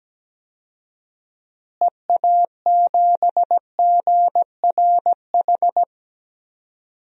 Morse code, heard as a keyed tone, about 17 wpm, 700 Hz. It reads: EA7GRH
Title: DXCC-100 Call Signs - 6 Characters - Repeated 2x - 17wpm